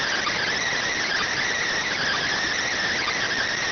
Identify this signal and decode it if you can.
Emergency Alert System (SAME) tones, heard as harsh buzzing bursts. It is heard from 0.8 to 3.4 s.